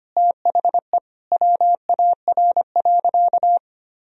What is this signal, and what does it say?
Morse code, heard as a keyed tone, about 25 wpm, 700 Hz. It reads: THE WAR.
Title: Sentences from Top 1000 Words - Review Part 4 - 25wpm